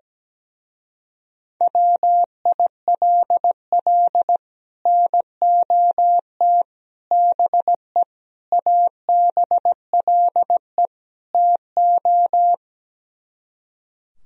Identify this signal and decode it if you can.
Morse code, heard as a keyed tone, about 17 wpm, 700 Hz. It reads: WILL NOT BE ABLE TO
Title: Speed-Racing - Sets of 5 Words 17wpm